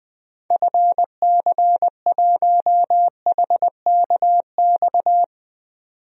Morse code, heard as a keyed tone, about 20 wpm, 700 Hz. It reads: FC1HKX
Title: DXCC-100 Call Signs - 6 Characters - 20wpm